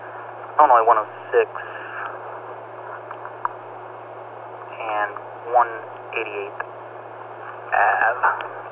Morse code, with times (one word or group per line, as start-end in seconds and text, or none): none